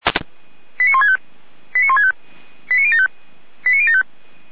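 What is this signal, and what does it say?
Five-tone selective call (ZVEI), heard as quick runs of five tones, tones of about 70 ms. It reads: 89165 89165 89075 89075
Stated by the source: Five-tone ZVEI selective call received on an SDR.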